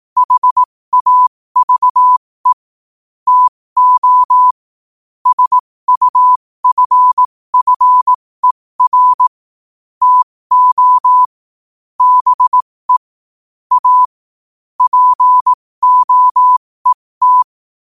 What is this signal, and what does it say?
Morse code, heard as a keyed tone, about 18 wpm, spaced as for 15 wpm, 1000 Hz. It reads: HAVE TO SUFFER TO BE A POET
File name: qotdfiles / 15wpm